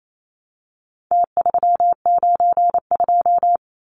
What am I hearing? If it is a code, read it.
Morse code, heard as a keyed tone, about 28 wpm, 700 Hz. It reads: T392
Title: Numbers - Contest Style - 4-Digit 28wpm